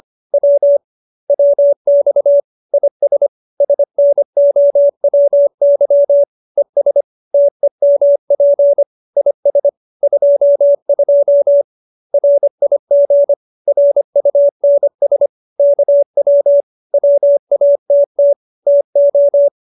Morse code, heard as a keyed tone, about 25 wpm, 570 Hz. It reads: W WX IS SNOWY ES TEMP IS 22 RIG RUNS KW WATT TO